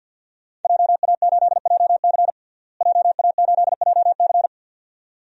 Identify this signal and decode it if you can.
Morse code, heard as a keyed tone, about 51 wpm, 700 Hz. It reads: JA8JC JA8JC
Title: DXCC-100 Call Signs - 5 Characters - Repeated 2x - 50wpm